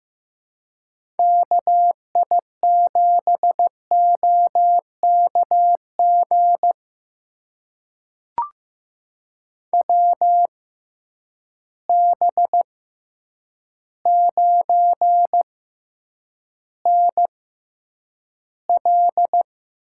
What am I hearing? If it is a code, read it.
Morse code, heard as a keyed tone, about 15 wpm, spaced as for 8 wpm, 700 Hz. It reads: KI7OKG WB9NL
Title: USA Call Signs - 6 Characters - 6x Character Spacing - 15wpm